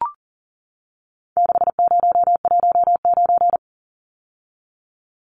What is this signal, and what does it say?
Morse code, heard as a keyed tone, about 40 wpm, 700 Hz. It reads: 6019
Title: Numbers - 4-Digit 40wpm